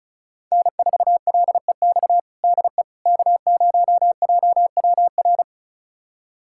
Morse code, heard as a keyed tone, about 36 wpm, 690 Hz. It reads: N4LEX DE K0JWR
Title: USA Callsign DE Callsign - 5 Characters - 35wpm